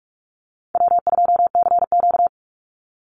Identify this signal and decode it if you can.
Morse code, heard as a keyed tone, about 44 wpm, 700 Hz. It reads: W2CQ
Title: Easy Call Signs Repeated 3x - 45wpm